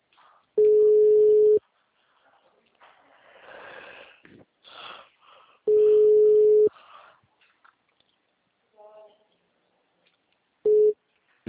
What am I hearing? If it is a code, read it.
Ringback tone (Europe, 425 Hz), with speech.